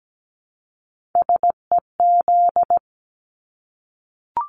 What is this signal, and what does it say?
Morse code, heard as a keyed tone, about 17 wpm, 700 Hz. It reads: SEZ
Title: Abbreviations Repeated 3x - 17wpm